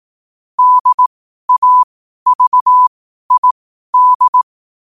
Morse code, heard as a keyed tone, about 18 wpm, spaced as for 12 wpm, 1000 Hz. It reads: DAVID
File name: qotdfiles / 12wpm